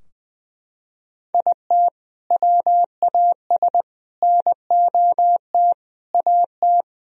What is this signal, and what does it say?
Morse code, heard as a keyed tone, about 21 wpm, 700 Hz. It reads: IT WAS NOT AT
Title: Sentences from Top 1000 Words - Review Part 4 - 20wpm